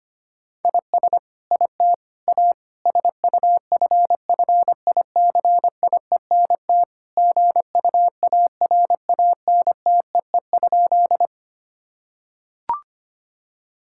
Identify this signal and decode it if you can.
Morse code, heard as a keyed tone, about 25 wpm, 700 Hz. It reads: IS IT A SUFFICIENT GUARANTEE?
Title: Sentences from Top 1600 Words 25wpm